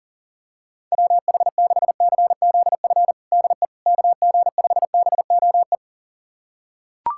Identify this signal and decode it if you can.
Morse code, heard as a keyed tone, about 40 wpm, 700 Hz. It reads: WH6CZF DE KG5BOE
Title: USA Callsign DE Callsign - 6 Characters - 40wpm